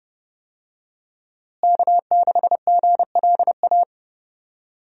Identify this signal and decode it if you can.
Morse code, heard as a keyed tone, about 31 wpm, 700 Hz. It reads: K6GLA
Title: USA Call Signs - 5 Characters - Repeated 3x - 30wpm